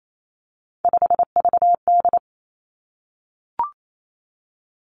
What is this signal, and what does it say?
Morse code, heard as a keyed tone, about 28 wpm, 700 Hz. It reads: 5VD
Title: Three Challenging Letter-Number Combinations 28wpm